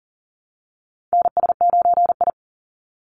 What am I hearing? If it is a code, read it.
Morse code, heard as a keyed tone, about 40 wpm, 700 Hz. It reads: NS9I
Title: Call Signs - 40wpm